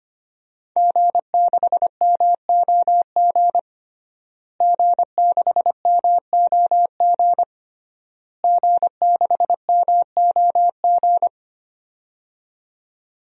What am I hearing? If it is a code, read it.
Morse code, heard as a keyed tone, about 25 wpm, 700 Hz. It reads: G6MOG G6MOG G6MOG
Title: DXCC-100 Call Signs - 5 Characters - Repeated 3x - 25wpm